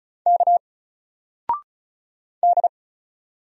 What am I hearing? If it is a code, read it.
Morse code, heard as a keyed tone, about 35 wpm, 700 Hz. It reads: K D
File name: Single-Letter-Number-35wpm